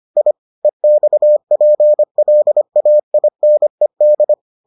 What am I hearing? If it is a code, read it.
Morse code, heard as a keyed tone, about 25 wpm, 600 Hz. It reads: I EXPLAINED